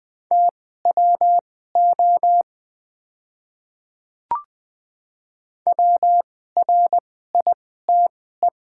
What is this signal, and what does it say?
Morse code, heard as a keyed tone, about 21 wpm, spaced as for 15 wpm, 700 Hz. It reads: TWO WRITE
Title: 2x Character Spacing - Top 200 Words - 20wpm